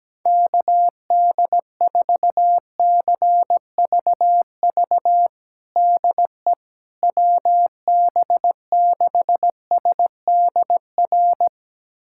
Morse code, heard as a keyed tone, about 17 wpm, 700 Hz. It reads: KD4CVV DE WB6SDR
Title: USA Callsign DE Callsign - 6 Characters - 17wpm